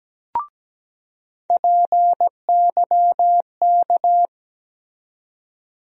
Morse code, heard as a keyed tone, about 17 wpm, 700 Hz. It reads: PYK
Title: USA Call Sign Postfixes - 3 Letters 17wpm